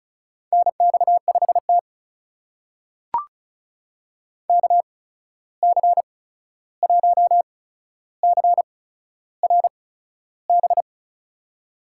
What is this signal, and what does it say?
Morse code, heard as a keyed tone, about 34 wpm, spaced as for 13 wpm, 700 Hz. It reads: NX5T KC1CRB
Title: USA Call Signs - 8x Character Spacing - 35wpm